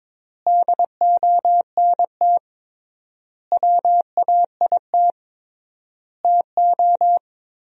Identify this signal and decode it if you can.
Morse code, heard as a keyed tone, about 22 wpm, 700 Hz. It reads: DONT WAIT TO
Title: 3x Word-Spacing - Sentences from Top 500 Words - 22wpm